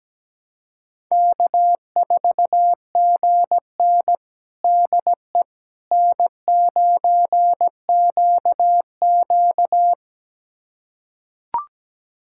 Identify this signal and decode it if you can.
Morse code, heard as a keyed tone, about 17 wpm, 700 Hz. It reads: K4GN DE N9QQ
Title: USA Callsign DE Callsign - 4 Characters - 17wpm